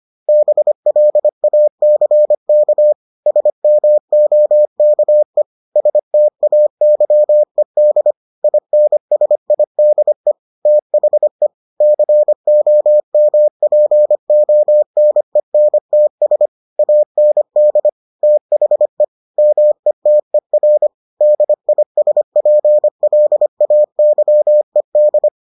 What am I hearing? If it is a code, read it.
Morse code, heard as a keyed tone, about 25 wpm, 600 Hz. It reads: BLACK SMOKE STAYED INSIDE THE COMPONENTS AND THE METER DISPLAYED